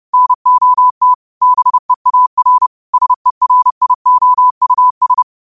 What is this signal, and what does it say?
Morse code, heard as a keyed tone, about 30 wpm, 1000 Hz. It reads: NOT BEAR SERIOUS
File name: 30wpm